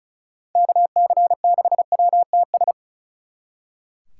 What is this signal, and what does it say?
Morse code, heard as a keyed tone, about 45 wpm, 700 Hz. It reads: KC6WTS